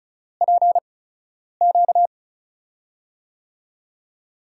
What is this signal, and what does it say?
Morse code, heard as a keyed tone, about 36 wpm, spaced as for 10 wpm, 700 Hz. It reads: PQ
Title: USA Call Signs - 6 Characters - 8x Character Spacing - 35wpm